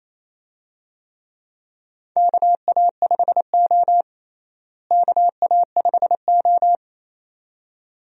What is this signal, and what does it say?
Morse code, heard as a keyed tone, about 28 wpm, 700 Hz. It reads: KA5O KA5O
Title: USA Call Signs - 4 Characters - Repeated 2x - 28wpm